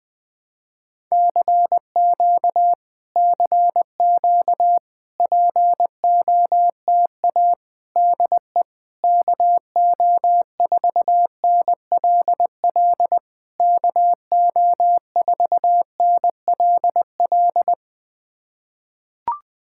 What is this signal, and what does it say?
Morse code, heard as a keyed tone, about 20 wpm, 700 Hz. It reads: CQ CQ POTA DE KO4NLL KO4NLL
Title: POTA QSO Practice - 20wpm